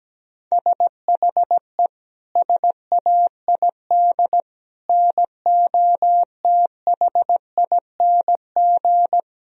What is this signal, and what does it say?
Morse code, heard as a keyed tone, about 17 wpm, 700 Hz. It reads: SHE SAID NOTHING
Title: Sentences from Top 1000 Words - Review Part 2 - 17wpm